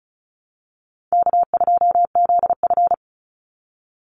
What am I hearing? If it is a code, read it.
Morse code, heard as a keyed tone, about 34 wpm, 700 Hz. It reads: K2ZF
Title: Call Signs - 35wpm